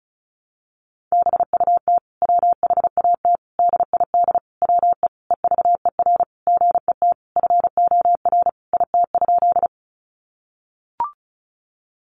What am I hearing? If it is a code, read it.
Morse code, heard as a keyed tone, about 34 wpm, 700 Hz. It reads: BUT WHAT DID WE EVER GET FOR IT?
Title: Sentences from Top 400 Words 35wpm